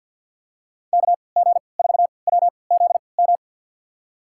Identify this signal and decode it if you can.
Morse code, heard as a keyed tone, about 51 wpm, spaced as for 28 wpm, 700 Hz. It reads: KG4WZM
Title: USA Call Signs - 6 Characters - 3x Character Spacing - 50wpm